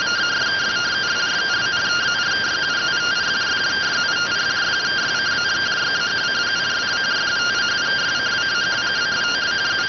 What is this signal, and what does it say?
Two-tone teleprinter signal, about 170 Hz shift.